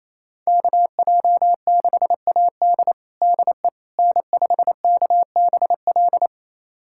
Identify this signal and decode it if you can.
Morse code, heard as a keyed tone, about 28 wpm, 700 Hz. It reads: KJ6AD DE N5KBL
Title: USA Callsign DE Callsign - 5 Characters - 28wpm